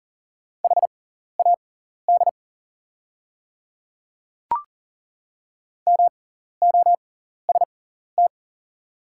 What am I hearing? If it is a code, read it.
Morse code, heard as a keyed tone, about 40 wpm, spaced as for 14 wpm, 690 Hz. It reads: HAD MOST
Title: 6x Character Spacing - Top 500 Words - 40wpm